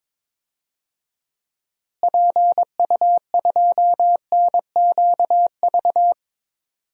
Morse code, heard as a keyed tone, about 22 wpm, 700 Hz. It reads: PU2NQV
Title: DXCC-100 Call Signs - 6 Characters - Repeated 3x - 22wpm